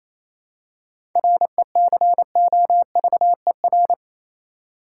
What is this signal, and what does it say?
Morse code, heard as a keyed tone, about 28 wpm, 700 Hz. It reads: RECOVER
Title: Seven Letter Words - 28wpm